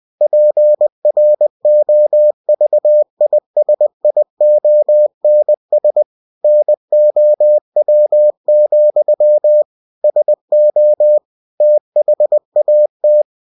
Morse code, heard as a keyed tone, about 20 wpm, 600 Hz. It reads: PROVISIONS NOW, SO THAT
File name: cw-books / winnie-the-pooh / winnie-20wpm-page-0083